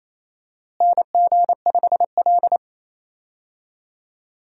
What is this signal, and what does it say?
Morse code, heard as a keyed tone, about 28 wpm, 700 Hz. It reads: NG5L
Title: USA Call Signs - 4 Characters - Repeated 2x - 28wpm